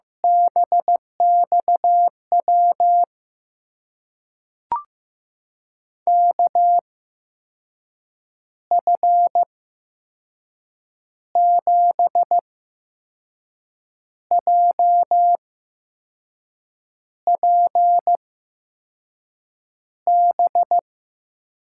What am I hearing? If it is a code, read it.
Morse code, heard as a keyed tone, about 15 wpm, spaced as for 5 wpm, 700 Hz. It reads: BX W KF7JPB
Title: USA Call Signs - 6 Characters - 8x Character Spacing - 15wpm